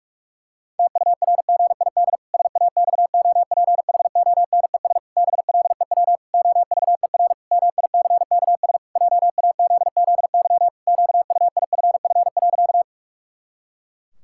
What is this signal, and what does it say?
Morse code, heard as a keyed tone, about 46 wpm, 700 Hz. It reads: TURGID SAXOPHONES BLEW OVER MICKS JAZZY QUIFF.
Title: Pangram Challenge v2 - 45wpm